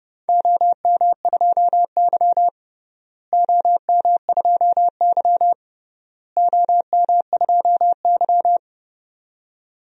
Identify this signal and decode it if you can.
Morse code, heard as a keyed tone, about 30 wpm, 700 Hz. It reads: OM2Y OM2Y OM2Y
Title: DXCC-100 Call Signs - 4 Characters - Repeated 3x - 30wpm